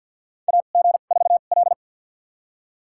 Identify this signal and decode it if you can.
Morse code, heard as a keyed tone, about 50 wpm, spaced as for 36 wpm, 700 Hz. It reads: AG4L
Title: USA Call Signs - 4 Characters - 2x Character Spacing - 50wpm